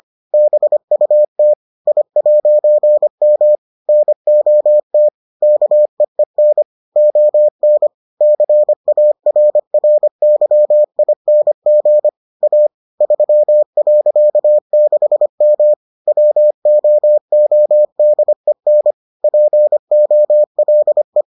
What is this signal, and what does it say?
Morse code, heard as a keyed tone, about 25 wpm, 600 Hz. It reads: BUT I'M NOT KEEN ON CARRYING A 3.6M WOODEN POLE